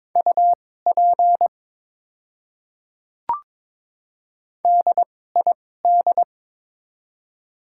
Morse code, heard as a keyed tone, about 22 wpm, spaced as for 16 wpm, 700 Hz. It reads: UP DID
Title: 2x Character Spacing - Top 100 Words - 22wpm